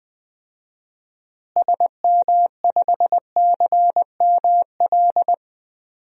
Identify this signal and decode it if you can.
Morse code, heard as a keyed tone, about 20 wpm, 700 Hz. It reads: SM5CML
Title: DXCC-100 Call Signs - 6 Characters - Repeated 3x - 20wpm